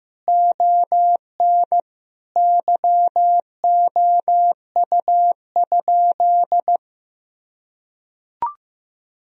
Morse code, heard as a keyed tone, about 15 wpm, 700 Hz. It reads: ON YOU?
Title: Sentences from Top 1000 Words - Review Part 3 - 15wpm